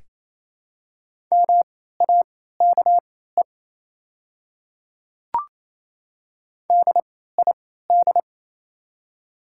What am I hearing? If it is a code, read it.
Morse code, heard as a keyed tone, about 27 wpm, spaced as for 16 wpm, 700 Hz. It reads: MAKE DID
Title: 3x Character Spacing - Top 200 Words - 28wpm